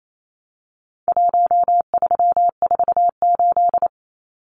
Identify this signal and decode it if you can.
Morse code, heard as a keyed tone, about 28 wpm, 700 Hz. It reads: 1348
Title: Numbers - Contest Style - 4-Digit 28wpm